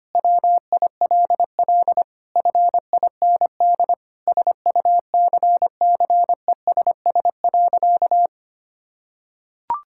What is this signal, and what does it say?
Morse code, heard as a keyed tone, about 25 wpm, 700 Hz. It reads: WILL FIND SUCCESS.
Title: Sentences from Top 1000 Words - Review Part 6 - 25wpm